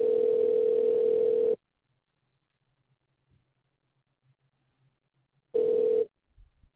Ringback tone (North America), 440 + 480 Hz.